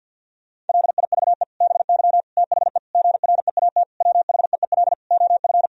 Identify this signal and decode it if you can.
Morse code, heard as a keyed tone, about 50 wpm, 700 Hz. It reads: LIVE BY THE GREAT WHEEL OF
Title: Sentences from Top 1000 Words - Review Part 3 - 50wpm